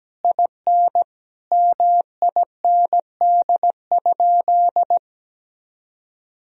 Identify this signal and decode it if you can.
Morse code, heard as a keyed tone, about 17 wpm, 700 Hz. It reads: IN MIND?
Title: Sentences from Top 1000 Words - Review Part 3 - 17wpm